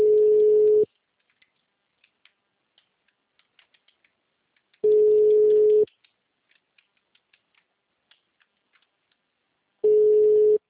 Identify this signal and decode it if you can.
Ringback tone (Europe, 425 Hz).